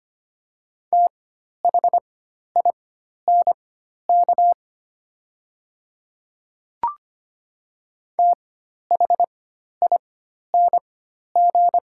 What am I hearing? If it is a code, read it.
Morse code, heard as a keyed tone, about 25 wpm, spaced as for 12 wpm, 700 Hz. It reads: THINK THING
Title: 4x Character Spacing - Top 200 Words - 25wpm